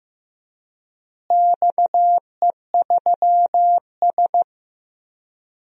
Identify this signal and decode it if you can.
Morse code, heard as a keyed tone, about 15 wpm, 700 Hz. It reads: XE3S
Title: DXCC-100 Call Signs - 4 Characters - Repeated 3x - 15wpm